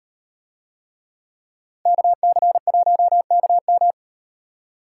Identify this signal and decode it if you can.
Morse code, heard as a keyed tone, about 36 wpm, 700 Hz. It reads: KC1KM